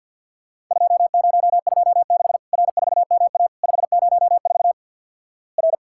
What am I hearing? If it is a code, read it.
Morse code, heard as a keyed tone, about 49 wpm, 700 Hz. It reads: 2026 R4MA 504 R
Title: Worked All Europe DX Contest - QTC Exchange with Repeats - 50wpm